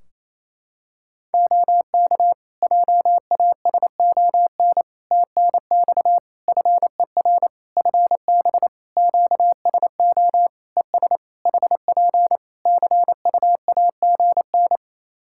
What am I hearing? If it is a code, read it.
Morse code, heard as a keyed tone, about 27 wpm, 700 Hz. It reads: OK JASON TNX FER FB QSO ES HP CUAGN
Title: Practice QSO - Exchange 4 - Ending - 28wpm